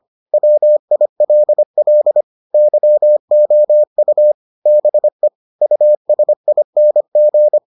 Morse code, heard as a keyed tone, about 25 wpm, 600 Hz. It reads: WILL YOU BE USING